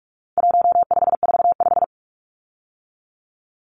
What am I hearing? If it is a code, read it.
Morse code, heard as a keyed tone, about 44 wpm, 700 Hz. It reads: 1545
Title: Numbers - 4-Digit 45wpm